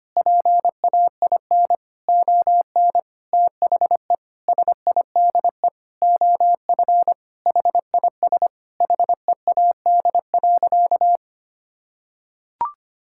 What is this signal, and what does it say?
Morse code, heard as a keyed tone, about 25 wpm, 700 Hz. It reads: PAIN ON THE SIDE OF HIS HEAD.
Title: Sentences from Top 1600 Words 25wpm